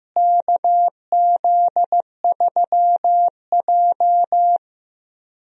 Morse code, heard as a keyed tone, about 15 wpm, 700 Hz. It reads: KZ3J
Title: USA Call Signs - 4 Characters - Repeated 3x - 15wpm